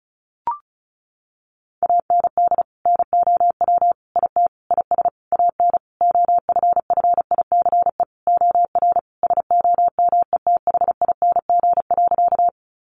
Morse code, heard as a keyed tone, about 34 wpm, 700 Hz. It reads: AND NOW IT IS AN OFFICE OR SOMETHING.
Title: Sentences from Top 1000 Words 35wpm